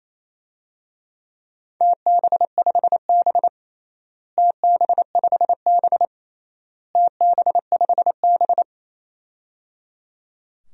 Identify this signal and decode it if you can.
Morse code, heard as a keyed tone, about 28 wpm, 700 Hz. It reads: TB5B TB5B TB5B
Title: DXCC-100 Call Signs - 4 Characters - Repeated 3x - 28wpm